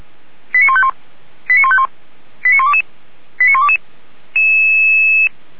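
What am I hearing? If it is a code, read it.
Five-tone selective call (ZVEI), heard as quick runs of five tones, tones of about 70 ms. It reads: 89151 89151 89122 89122
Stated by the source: ZVEI five-tone call 89247 received by SDR, fire brigade Feuerwehr Orlamünde.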